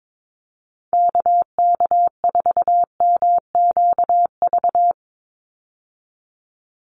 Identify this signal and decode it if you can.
Morse code, heard as a keyed tone, about 22 wpm, 700 Hz. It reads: KK4MQV
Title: USA Call Signs 22wpm